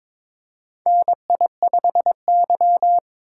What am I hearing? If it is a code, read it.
Morse code, heard as a keyed tone, about 22 wpm, 700 Hz. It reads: NI5Y
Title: USA Call Signs - 4 Characters - 8x Character Spacing - 22wpm